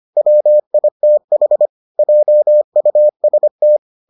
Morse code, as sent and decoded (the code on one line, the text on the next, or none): .-- .. - .... / .--- ..- ... -
WITH JUST